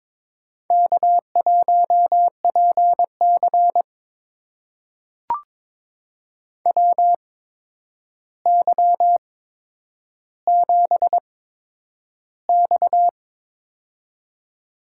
Morse code, heard as a keyed tone, about 22 wpm, spaced as for 9 wpm, 700 Hz. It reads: K1PC WY7X
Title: USA Call Signs - 4 Characters - 8x Character Spacing - 22wpm